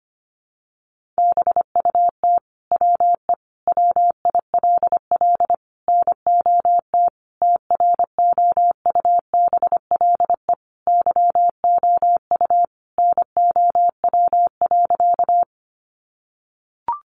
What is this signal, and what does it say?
Morse code, heard as a keyed tone, about 25 wpm, 700 Hz. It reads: BUT WE WILL NOT TROUBLE YOU NOW.
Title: Sentences from Top 1200 Words 25wpm